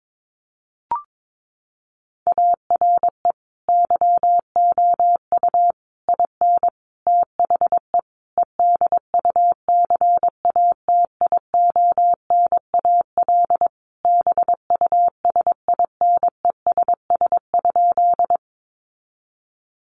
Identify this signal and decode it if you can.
Morse code, heard as a keyed tone, about 22 wpm, 700 Hz. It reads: ARE YOU IN THE EDUCATIONAL BUSINESS?
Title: Sentences from Top 1200 Words 22wpm